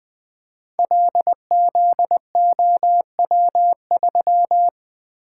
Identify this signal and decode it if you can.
Morse code, heard as a keyed tone, about 20 wpm, 700 Hz. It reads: LZOW3
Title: Code Groups - 5 Letter:Numbers - 20wpm